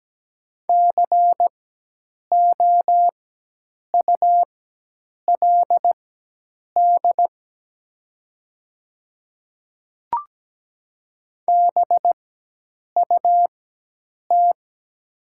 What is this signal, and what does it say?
Morse code, heard as a keyed tone, about 17 wpm, spaced as for 8 wpm, 700 Hz. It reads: COULD BUT
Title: 4x Character Spacing - Top 100 Words - 17wpm